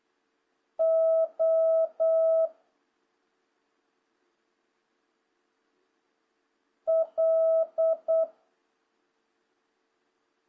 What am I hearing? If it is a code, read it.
Morse code, heard as a keyed tone, about 8 wpm, 650 Hz. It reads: O L
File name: jh8cbh / 2026cw